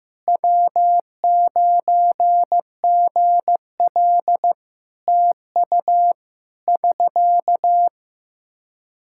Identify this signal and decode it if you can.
Morse code, heard as a keyed tone, about 15 wpm, 700 Hz. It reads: W9GL TU <SK>
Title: K1USN SST QSO Practice 15wpm